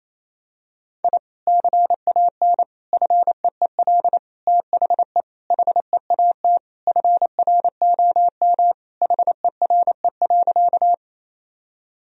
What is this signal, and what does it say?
Morse code, heard as a keyed tone, about 28 wpm, 700 Hz. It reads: I CAN FEEL THE HEAT FROM HERE.
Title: Sentences from Top 1000 Words - Review Part 3 - 28wpm